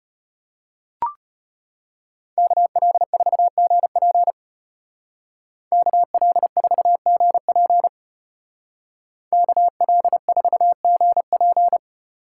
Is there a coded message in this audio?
Yes: Morse code from 2.4 to 11.8 s, heard as a keyed tone, about 33 wpm, 700 Hz. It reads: KL4GP KL4GP KL4GP